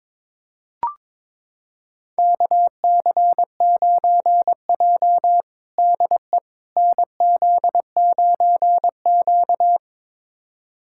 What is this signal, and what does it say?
Morse code, heard as a keyed tone, about 22 wpm, 700 Hz. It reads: KC9J DE NZ9Q
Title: USA Callsign DE Callsign - 4 Characters - 22wpm